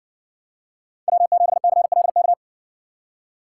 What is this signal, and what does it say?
Morse code, heard as a keyed tone, about 60 wpm, 700 Hz. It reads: W7CLK